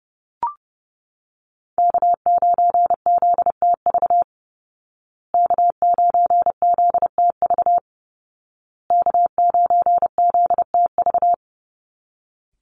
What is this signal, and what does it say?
Morse code, heard as a keyed tone, about 30 wpm, 700 Hz. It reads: K9ZTV K9ZTV K9ZTV
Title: Call Signs Repeated 2x - 30wpm